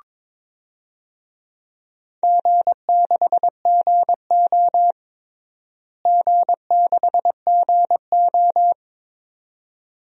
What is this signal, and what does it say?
Morse code, heard as a keyed tone, about 22 wpm, 700 Hz. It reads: G6GO G6GO
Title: DXCC-100 Call Signs - 4 Characters - Repeated 2x - 22wpm